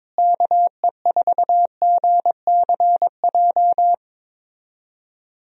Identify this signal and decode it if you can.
Morse code, heard as a keyed tone, about 22 wpm, 700 Hz. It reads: KE4GCJ
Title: USA Call Signs - 6 Characters - Repeated 3x - 22wpm